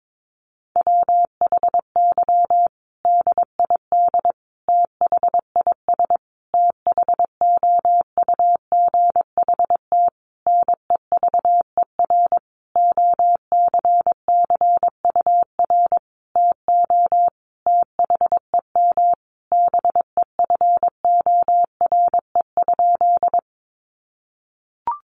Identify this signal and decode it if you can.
Morse code, heard as a keyed tone, about 22 wpm, 700 Hz. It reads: WHY DID THIS THOUGHT NEVER OCCUR TO THEM BEFORE?
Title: Sentences from Top 900 Words 22wpm